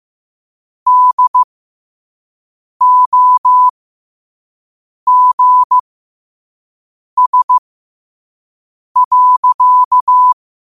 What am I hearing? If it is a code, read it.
Morse code, heard as a keyed tone, about 15 wpm, spaced as for 5 wpm, 1000 Hz. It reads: DOGS.
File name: qotdfiles / 5wpm